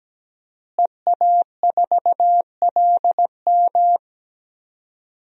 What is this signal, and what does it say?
Morse code, heard as a keyed tone, about 17 wpm, 700 Hz. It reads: EA4LM
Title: DXCC-100 Call Signs - 5 Characters - Repeated 3x - 17wpm